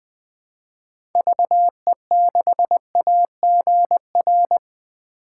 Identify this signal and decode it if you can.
Morse code, heard as a keyed tone, about 20 wpm, 700 Hz. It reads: VE6AGR
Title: 2023 SOTA Chaser Call Signs - 20wpm